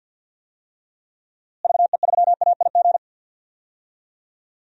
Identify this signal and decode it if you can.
Morse code, heard as a keyed tone, about 46 wpm, 700 Hz. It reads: VE3AIG
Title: DXCC-100 Call Signs - 6 Characters - Repeated 3x - 50wpm